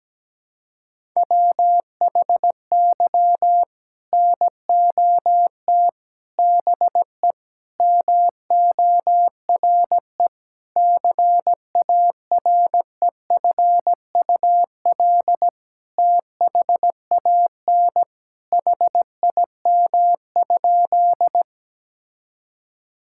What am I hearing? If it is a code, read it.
Morse code, heard as a keyed tone, about 17 wpm, 700 Hz. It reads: WHY NOT BE MORE CAREFUL THAN HIM?
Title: Sentences from Top 1600 Words 17wpm